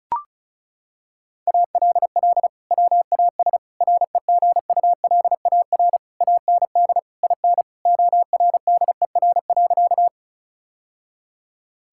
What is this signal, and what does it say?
Morse code, heard as a keyed tone, about 34 wpm, 700 Hz. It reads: ALL WAS REGULAR AND IN ORDER.
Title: Sentences from Top 1000 Words - Review Part 6 - 35wpm